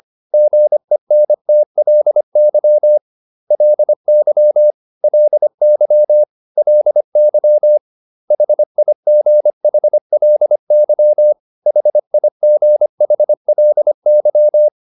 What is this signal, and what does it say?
Morse code, heard as a keyed tone, about 25 wpm, 600 Hz. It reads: GENTLY LY LY LY HIGHLY HIGHLY